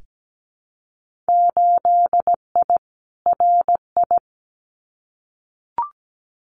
Morse code, heard as a keyed tone, about 17 wpm, 700 Hz. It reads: 8I RI
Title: Winter Field Day Exchanges Encore 17wpm